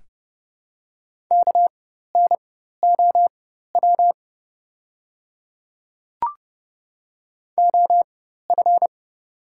Morse code, heard as a keyed tone, about 30 wpm, spaced as for 14 wpm, 700 Hz. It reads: KNOW OF
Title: 4x Character Spacing - Top 200 Words - 30wpm